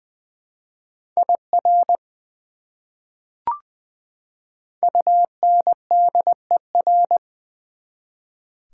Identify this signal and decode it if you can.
Morse code, heard as a keyed tone, about 20 wpm, 700 Hz. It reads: IR UNDER
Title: Prefixes and Suffixes 20wpm